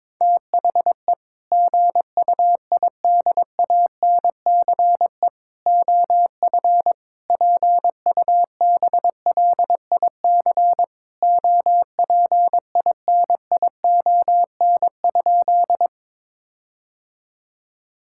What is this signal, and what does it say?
Morse code, heard as a keyed tone, about 22 wpm, 700 Hz. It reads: THE GUIDANCE OF PUBLIC OPINION?
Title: Sentences from Top 1600 Words 22wpm